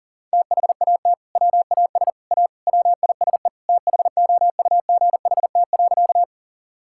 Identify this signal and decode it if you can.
Morse code, heard as a keyed tone, about 40 wpm, 700 Hz. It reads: THAT WAS A WISE THOUGHT.
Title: Sentences from Top 1600 Words 40wpm